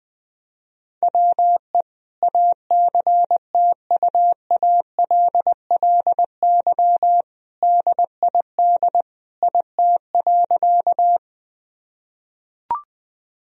Morse code, heard as a keyed tone, about 21 wpm, 700 Hz. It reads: WE ACTUALLY DID IT.
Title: Sentences from Top 1000 Words - Review Part 4 - 20wpm